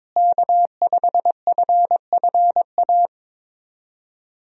Morse code, heard as a keyed tone, about 22 wpm, 700 Hz. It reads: K5FFA
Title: USA Call Signs - 5 Characters - Repeated 3x - 22wpm